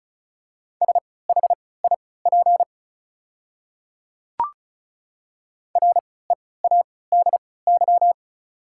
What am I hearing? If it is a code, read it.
Morse code, heard as a keyed tone, about 34 wpm, spaced as for 20 wpm, 690 Hz. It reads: SHIP READY
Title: 3x Character Spacing - Top 500 Words - 35wpm